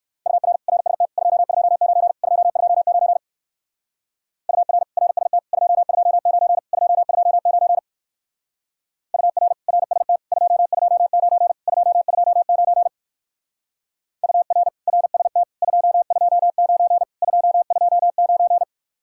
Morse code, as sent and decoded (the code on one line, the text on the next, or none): ..- .-. / .-. ... - / ..--- ..--- ----. / ..--- ..--- ----. / ..- .-. / .-. ... - / ..--- ..--- ----. / ..--- ..--- ----. / ..- .-. / .-. ... - / ..--- ..--- ----. / ..--- ..--- ----. / ..- .-. / .-. ... - / ..--- ..--- ----. / ..--- ..--- ----.
UR RST 229 229 UR RST 229 229 UR RST 229 229 UR RST 229 229